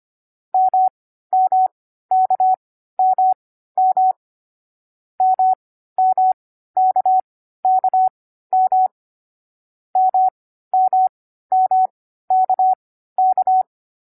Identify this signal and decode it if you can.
Morse code, heard as a keyed tone, about 25 wpm, spaced as for 14 wpm, 750 Hz. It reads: MMKMM MMKKM MMMKK
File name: CW-Koch / koch.2.4.groups5.15wpm